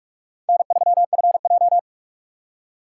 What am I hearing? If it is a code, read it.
Morse code, heard as a keyed tone, about 56 wpm, 700 Hz. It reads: N3FJ
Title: Speed Racing - USA 1x2 Call Signs 45wpm